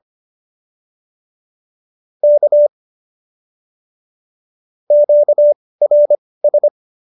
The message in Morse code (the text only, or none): K QRS